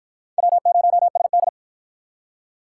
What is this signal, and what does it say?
Morse code, heard as a keyed tone, about 53 wpm, 700 Hz. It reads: W0SD